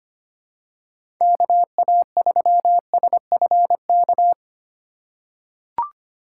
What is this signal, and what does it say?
Morse code, heard as a keyed tone, about 25 wpm, 700 Hz. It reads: KA3SFK
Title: USA Call Signs - 3x Character Spacing - 25wpm